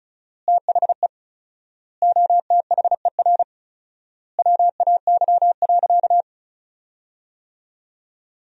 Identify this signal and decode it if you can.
Morse code, heard as a keyed tone, about 36 wpm, 700 Hz. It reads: THE OTHER WAY.
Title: 4x Word-Spacing - Sentences from Top 100 Words - 35wpm